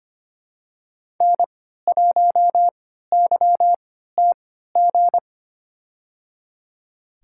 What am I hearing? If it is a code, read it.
Morse code, heard as a keyed tone, about 25 wpm, spaced as for 14 wpm, 700 Hz. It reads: N1YTG
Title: USA Call Signs - 5 Characters - 3x Character Spacing - 25wpm